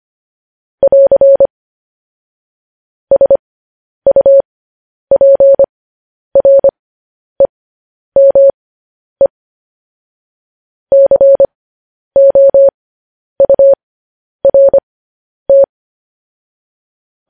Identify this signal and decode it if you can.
Morse code, heard as a keyed tone, about 25 wpm, spaced as for 10 wpm, 550 Hz. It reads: <AR> SUPREME COURT
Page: News Headlines in Morse Code at 10 WPM